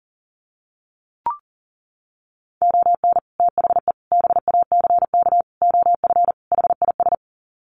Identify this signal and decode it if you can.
Morse code, heard as a keyed tone, about 37 wpm, 700 Hz. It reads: ON THE BACK OF HIS
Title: Sets of 5 Words Encore 40wpm